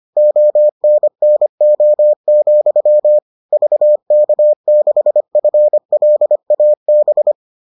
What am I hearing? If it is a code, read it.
Morse code, heard as a keyed tone, about 25 wpm, 600 Hz. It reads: ONNO, VK6FLAB